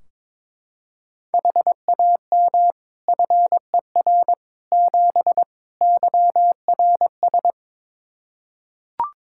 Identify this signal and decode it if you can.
Morse code, heard as a keyed tone, about 21 wpm, 700 Hz. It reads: HAM FER 7 YRS
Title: Practice QSO - Exchange 3 - More Info - 22wpm